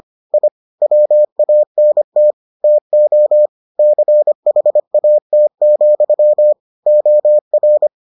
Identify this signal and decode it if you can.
Morse code, heard as a keyed tone, about 25 wpm, 600 Hz. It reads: I WANT TO CHAT, OR